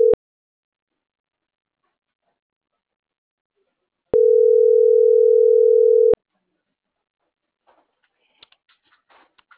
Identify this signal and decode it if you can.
Ringback tone (North America), 440 + 480 Hz.